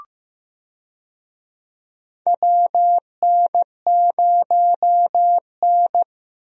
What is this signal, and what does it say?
Morse code, heard as a keyed tone, about 15 wpm, 700 Hz. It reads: WN0N
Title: USA Call Signs - 4 Characters - Repeated 2x - 15wpm